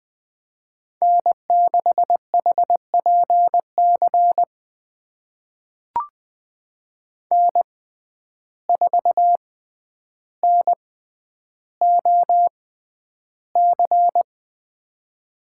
Morse code, heard as a keyed tone, about 20 wpm, spaced as for 10 wpm, 700 Hz. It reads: N6HPC N4NOC
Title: USA Call Signs - 5 Characters - 6x Character Spacing - 20wpm